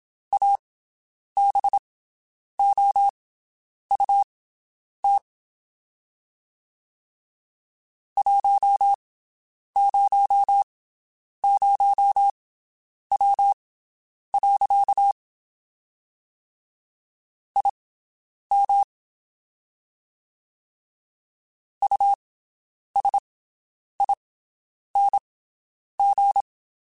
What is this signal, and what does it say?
Morse code, heard as a keyed tone, about 27 wpm, spaced as for 9 wpm, 800 Hz. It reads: ABOUT 100W. IM USING
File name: QSOs Spaced